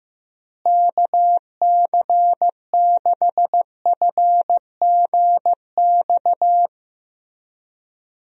Morse code, heard as a keyed tone, about 15 wpm, 700 Hz. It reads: KC6FGX
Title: USA Call Signs - 6 Characters - Repeated 3x - 15wpm